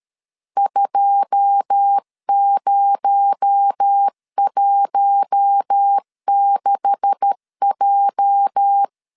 Morse code, heard as a keyed tone, about 13 wpm, 800 Hz. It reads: 2016J
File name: CW / DET / 1 9 1500C S A 12wpm 3800